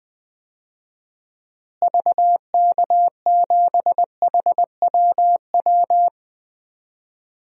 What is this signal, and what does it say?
Morse code, heard as a keyed tone, about 20 wpm, 700 Hz. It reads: VK7HWW
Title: DXCC-100 Call Signs - 6 Characters - Repeated 3x - 20wpm